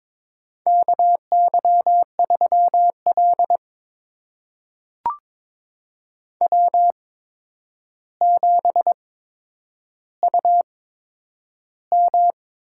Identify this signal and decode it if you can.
Morse code, heard as a keyed tone, about 22 wpm, spaced as for 9 wpm, 700 Hz. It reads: KY3L W7UM
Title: USA Call Signs - 4 Characters - 8x Character Spacing - 22wpm